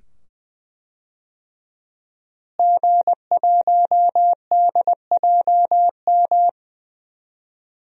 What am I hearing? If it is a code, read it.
Morse code, heard as a keyed tone, about 20 wpm, 700 Hz. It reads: G1DJM
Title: DXCC-100 Call Signs - 5 Characters - Repeated 2x - 20wpm